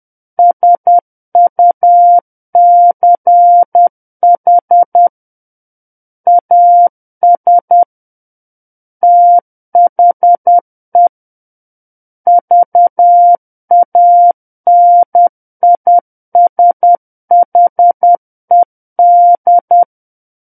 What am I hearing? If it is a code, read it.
Morse code, heard as a keyed tone, about 10 wpm, 700 Hz. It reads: SUCH AS THE VANISHED
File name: Book One / 01-Chapter 1 - 10 WPM